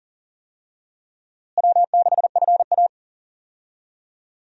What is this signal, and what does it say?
Morse code, heard as a keyed tone, about 40 wpm, 700 Hz. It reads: W6FA
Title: USA Call Signs - 4 Characters - Repeated 3x - 40wpm